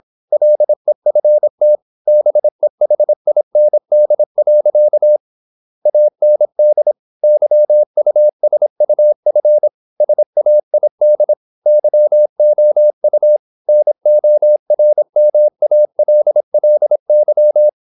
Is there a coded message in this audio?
Yes: Morse code from 0.3 to 17.7 s, heard as a keyed tone, about 26 wpm, 600 Hz. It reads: LEFT BEHIND. AND YUSUF SAID YOU NORMALLY